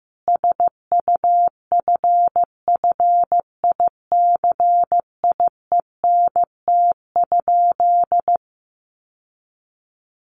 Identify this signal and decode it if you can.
Morse code, heard as a keyed tone, about 15 wpm, 700 Hz. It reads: SUFFICIENT?
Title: Sentences from Top 1400 Words 15wpm